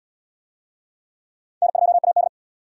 Morse code, heard as a keyed tone, about 66 wpm, 700 Hz. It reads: N2AD